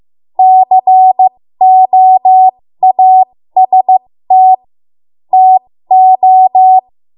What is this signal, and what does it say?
Morse code, heard as a keyed tone, about 15 wpm, spaced as for 13 wpm, 750 Hz. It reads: COAST TO
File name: Morse / 13WPM